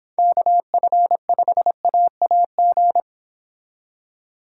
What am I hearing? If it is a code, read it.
Morse code, heard as a keyed tone, about 26 wpm, 700 Hz. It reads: KF5AAG